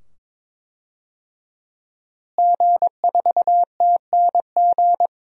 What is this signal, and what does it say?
Morse code, heard as a keyed tone, about 21 wpm, 700 Hz. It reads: G4TNG
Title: DXCC-100 Call Signs - 5 Characters - Repeated 2x - 22wpm